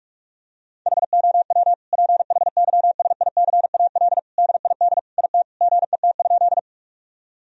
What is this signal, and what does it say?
Morse code, heard as a keyed tone, about 41 wpm, 700 Hz. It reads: HOW PHYSICAL DID IT GET?
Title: Sentences from Top 1000 Words - Review Part 4 - 45wpm